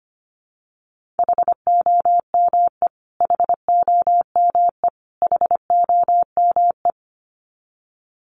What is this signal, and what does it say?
Morse code, heard as a keyed tone, about 25 wpm, 700 Hz. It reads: HOME HOME HOME
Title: Top 500 Words Repeated 25wpm